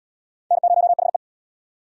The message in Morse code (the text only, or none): N0HE